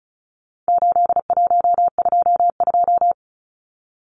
Morse code, heard as a keyed tone, about 34 wpm, 700 Hz. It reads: 8122
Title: Numbers - 4-Digit 35wpm